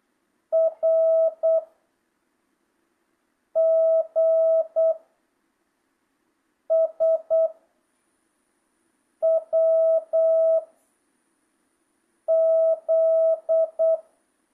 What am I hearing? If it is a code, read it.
Morse code, heard as a keyed tone, about 8 wpm, 1300 Hz. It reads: R G S W Z
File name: jh8cbh / 2025cw